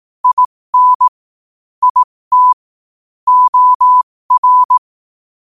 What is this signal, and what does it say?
Morse code, heard as a keyed tone, about 18 wpm, spaced as for 15 wpm, 1000 Hz. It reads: IN IT OR
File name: qotdfiles / 15wpm